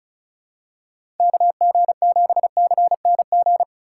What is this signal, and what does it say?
Morse code, heard as a keyed tone, about 36 wpm, 700 Hz. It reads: KG7CNG
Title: USA Call Signs - 2x Character Spacing - 35wpm